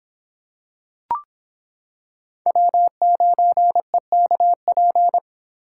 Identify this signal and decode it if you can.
Morse code, heard as a keyed tone, about 25 wpm, 700 Hz. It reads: W9EKP